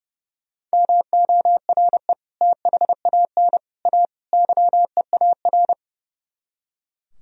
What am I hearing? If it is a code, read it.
Morse code, heard as a keyed tone, about 37 wpm, 700 Hz. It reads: MORE THAN A YEAR
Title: Speed-Racing - Sets of 4 Words 30wpm